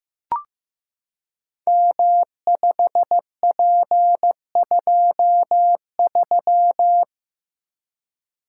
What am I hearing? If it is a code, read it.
Morse code, heard as a keyed tone, about 15 wpm, 700 Hz. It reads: M5P23
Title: Code Groups - 5 Letter:Numbers - 15wpm